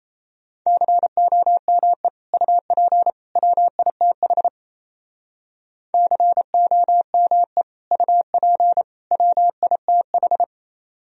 Morse code, heard as a keyed tone, about 34 wpm, 700 Hz. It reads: COME UP WITH COME UP WITH
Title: Speed-Racing - Sets of 3 Words 28wpm